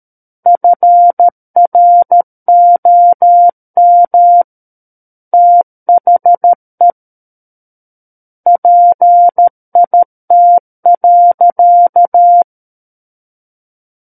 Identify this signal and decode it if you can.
Morse code, heard as a keyed tone, about 13 wpm, 700 Hz. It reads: FROM THE PIT.
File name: Book One / 04-Chapter 4 - 13 WPM